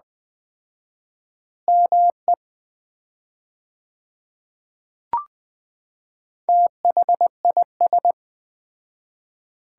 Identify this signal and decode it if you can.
Morse code, heard as a keyed tone, about 20 wpm, 700 Hz. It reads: ME THIS
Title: 4x Word-Spacing - Sets of 4 Words - 20wpm